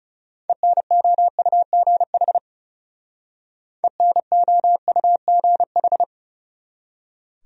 Morse code, heard as a keyed tone, about 38 wpm, 700 Hz. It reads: ENOUGH ENOUGH